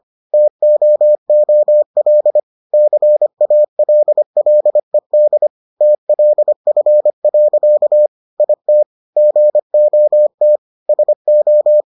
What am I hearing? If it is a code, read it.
Morse code, heard as a keyed tone, about 25 wpm, 600 Hz. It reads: TOOL CALLED TLF. IT GOT SO